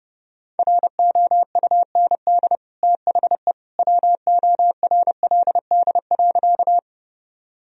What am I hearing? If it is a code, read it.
Morse code, heard as a keyed tone, about 32 wpm, 700 Hz. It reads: ROUND THE WORLD.
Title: Sentences from Top 1000 Words - Review Part 1 - 30wpm